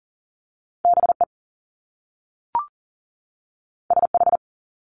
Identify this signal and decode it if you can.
Morse code, heard as a keyed tone, about 40 wpm, 700 Hz. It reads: BE SH
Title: Top Two Letter Combinations in Words 40wpm